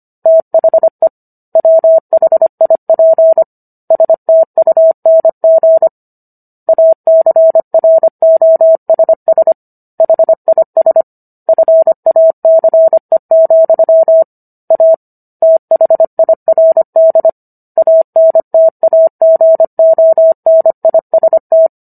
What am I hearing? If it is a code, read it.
Morse code, heard as a keyed tone, about 25 wpm, 650 Hz. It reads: THE WHIP STUNG ACROSS HIS FACE, A THIRD ANTAGONIST